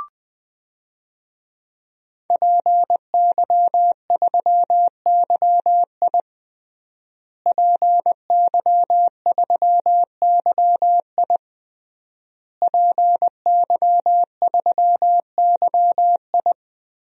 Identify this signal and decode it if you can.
Morse code, heard as a keyed tone, about 20 wpm, 700 Hz. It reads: PY3YI PY3YI PY3YI
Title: DXCC-100 Call Signs - 5 Characters - Repeated 3x - 20wpm